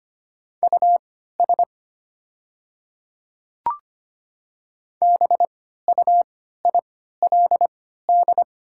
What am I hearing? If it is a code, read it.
Morse code, heard as a keyed tone, about 25 wpm, spaced as for 14 wpm, 700 Hz. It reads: US BUILD
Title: 3x Character Spacing - Top 300 Words - 25wpm